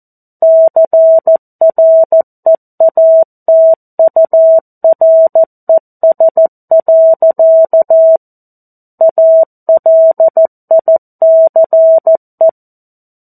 Morse code, heard as a keyed tone, about 14 wpm, 650 Hz. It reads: CREATURES. ALICE
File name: 02-Chapter-02 14 WPM